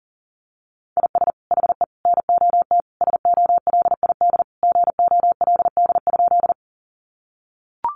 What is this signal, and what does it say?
Morse code, heard as a keyed tone, about 40 wpm, 700 Hz. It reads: IS HE NOT SOLID GOLD?
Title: Sentences from Top 1500 Words 40wpm